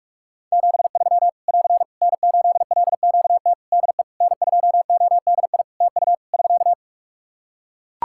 Morse code, heard as a keyed tone, about 43 wpm, 700 Hz. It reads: 73 <AR> N8LQT DE N2ODI TU <SK>
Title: K1USN SST QSO Practice 45wpm